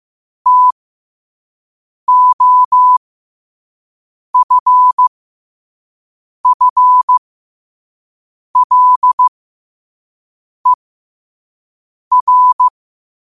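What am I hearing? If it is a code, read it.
Morse code, heard as a keyed tone, about 15 wpm, spaced as for 5 wpm, 1000 Hz. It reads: TOFFLER
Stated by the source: Morse practice at 5 wpm.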